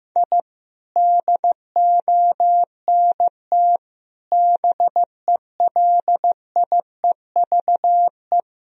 Morse code, heard as a keyed tone, about 15 wpm, 700 Hz. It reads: I DONT BELIEVE
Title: Sentences from Top 1000 Words - Review Part 1 - 15wpm